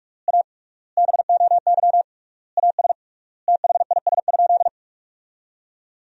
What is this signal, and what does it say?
Morse code, heard as a keyed tone, about 42 wpm, 700 Hz. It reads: A BOY AS THIS?
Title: 3x Word-Spacing - Sentences from Top 400 Words - 45wpm